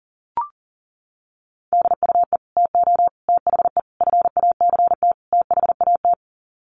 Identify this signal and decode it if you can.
Morse code, heard as a keyed tone, about 40 wpm, 700 Hz. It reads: DUE TO THE FACT THAT
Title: Sets of 5 Words Encore 40wpm